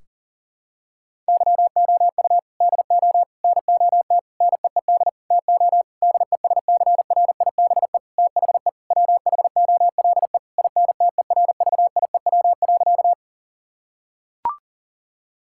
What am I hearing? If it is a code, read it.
Morse code, heard as a keyed tone, about 40 wpm, 700 Hz. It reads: YOU DO NOT NEED TO DESCRIBE THE WHOLE INTERVIEW.
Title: Sentences from Top 1600 Words 40wpm